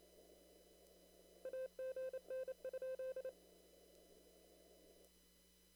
Morse code, heard as a keyed tone, about 28 wpm, 520 Hz. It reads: AGN?
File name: CW / kt5j wpx cw-op-k5pi